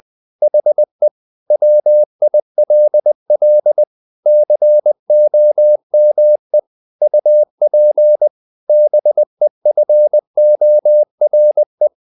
Morse code, heard as a keyed tone, about 20 wpm, 600 Hz. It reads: HE WILL COME UP BEFORE
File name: the old man and the sea / the old man and the sea part 1 20wpm